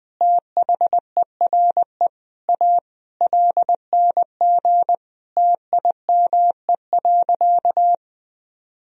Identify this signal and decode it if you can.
Morse code, heard as a keyed tone, about 21 wpm, 700 Hz. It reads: THERE A LNG TIME.
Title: Sentences from Top 400 Words and Abbreviations 20wpm